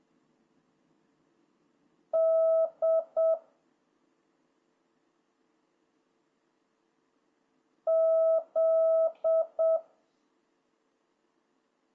Morse code, heard as a keyed tone, about 7 wpm, 650 Hz. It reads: D Z